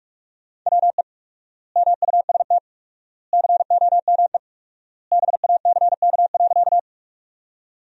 Morse code, heard as a keyed tone, about 47 wpm, 700 Hz. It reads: WE MUST COME BACK.
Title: 4x Word-Spacing - Sentences from Top 200 Words - 45wpm